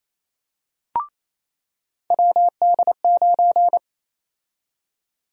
Morse code, heard as a keyed tone, about 27 wpm, 700 Hz. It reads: WD9
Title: USA Call Sign Prefixes - 28wpm